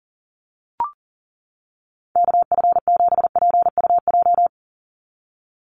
Morse code, heard as a keyed tone, about 40 wpm, 700 Hz. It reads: KF7PUJ
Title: USA Call Signs 40wpm